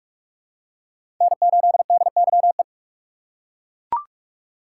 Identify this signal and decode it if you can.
Morse code, heard as a keyed tone, about 42 wpm, 700 Hz. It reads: N8DYE
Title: USA Call Signs - 5 Characters - 3x Character Spacing - 45wpm